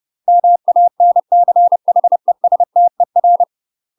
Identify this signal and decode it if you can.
Morse code, heard as a keyed tone, about 30 wpm, 700 Hz. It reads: MANCHESTER
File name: UK MASTER QSOs TRAINING / 30WPM / UK MASTER QSOs 0004